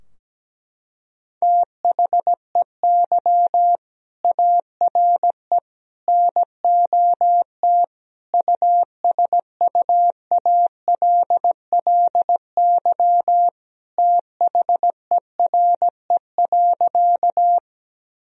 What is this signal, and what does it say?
Morse code, heard as a keyed tone, about 17 wpm, 700 Hz. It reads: THEY ARE NOT USUALLY THERE.
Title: Sentences from Top 1000 Words - Review Part 3 - 17wpm